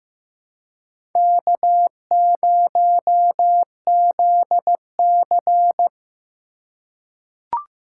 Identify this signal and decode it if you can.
Morse code, heard as a keyed tone, about 15 wpm, 700 Hz. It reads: K0ZC
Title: USA Call Signs - 4 Characters - 2x Character Spacing - 15wpm